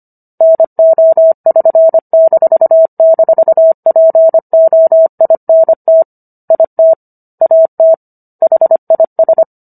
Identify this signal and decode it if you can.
Morse code, heard as a keyed tone, about 25 wpm, 650 Hz. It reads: NO<SN>--POINT IT AT HIS